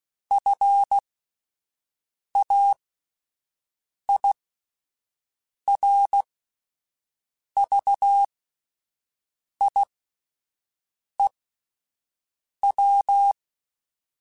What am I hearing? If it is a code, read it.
Morse code, heard as a keyed tone, about 16 wpm, spaced as for 5 wpm, 800 Hz. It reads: FAIRVIEW